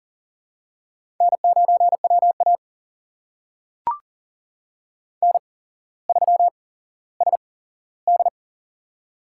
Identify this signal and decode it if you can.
Morse code, heard as a keyed tone, about 40 wpm, spaced as for 17 wpm, 700 Hz. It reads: N9WA N3SD
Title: USA Call Signs - 4 Characters - 8x Character Spacing - 40wpm